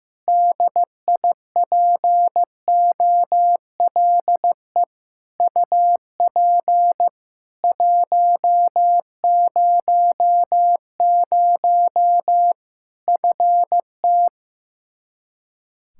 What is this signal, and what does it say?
Morse code, heard as a keyed tone, about 15 wpm, 700 Hz. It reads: DIPOLE UP 100 FT
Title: K1USN SST QSO Practice 15wpm